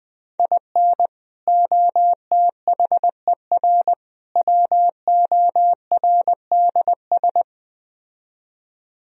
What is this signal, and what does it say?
Morse code, heard as a keyed tone, about 21 wpm, 700 Hz. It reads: IN OTHER WORDS